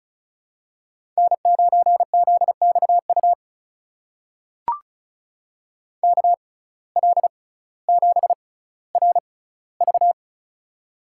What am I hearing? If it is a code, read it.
Morse code, heard as a keyed tone, about 36 wpm, spaced as for 18 wpm, 700 Hz. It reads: N9ZXU KL7RV
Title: USA Call Signs - 5 Characters - 6x Character Spacing - 35wpm